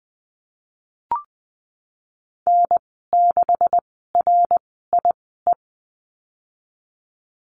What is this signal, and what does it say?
Morse code, heard as a keyed tone, about 20 wpm, spaced as for 15 wpm, 700 Hz. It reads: N6RIE
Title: 154 Lesson 25g - Letter p - Call Signs 20wpm